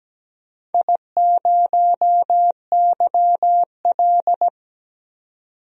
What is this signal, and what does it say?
Morse code, heard as a keyed tone, about 17 wpm, 700 Hz. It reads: I0YL
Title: DXCC-100 Call Signs - 4 Characters - 17wpm